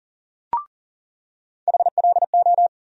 Morse code, heard as a keyed tone, about 37 wpm, 700 Hz. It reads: HLO
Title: USA Call Sign Postfixes - 3 Letters 40wpm